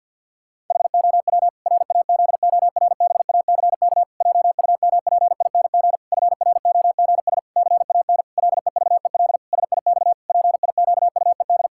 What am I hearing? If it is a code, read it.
Morse code, heard as a keyed tone, about 50 wpm, 700 Hz. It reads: HOW RAZORBACK JUMPING FROGS CAN LEVEL SIX PIQUED
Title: Pangram Challenge v2 - 50wpm